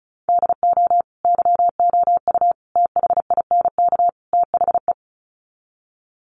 Morse code, heard as a keyed tone, about 34 wpm, 700 Hz. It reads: DO YOU THINK THE
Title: Sets of 4 Words 35wpm